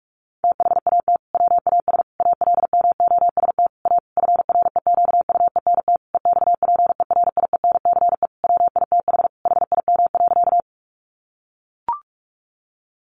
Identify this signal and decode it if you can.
Morse code, heard as a keyed tone, about 44 wpm, 700 Hz. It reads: THAT WAS ALMOST A FREQUENT EXPERIENCE WITH HIM.
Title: Sentences from Top 1400 Words 45wpm